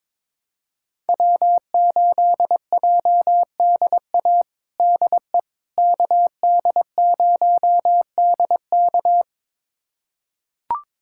Morse code, heard as a keyed tone, about 22 wpm, 700 Hz. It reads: W8JDA DE KD0DK
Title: USA Callsign DE Callsign - 5 Characters - 22wpm